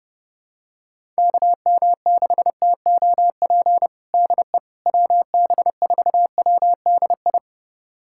Morse code, heard as a keyed tone, about 32 wpm, 700 Hz. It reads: KM6TOP DE WB4WDI
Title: USA Callsign DE Callsign - 6 Characters - 30wpm